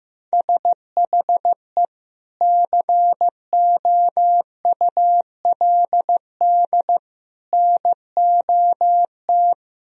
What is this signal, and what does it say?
Morse code, heard as a keyed tone, about 15 wpm, 700 Hz. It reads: SHE COULD NOT
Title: Sentences from Top 1000 Words - Review Part 2 - 15wpm